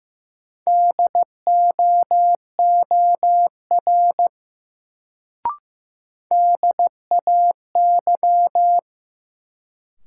Morse code, heard as a keyed tone, about 15 wpm, 700 Hz. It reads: DOOR DAY
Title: Sets of 2 Words - New Word Drill - 15wpm